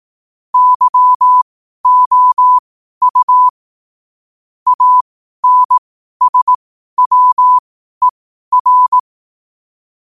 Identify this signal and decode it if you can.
Morse code, heard as a keyed tone, about 18 wpm, spaced as for 12 wpm, 1000 Hz. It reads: YOU ANSWER
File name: qotdfiles / 12wpm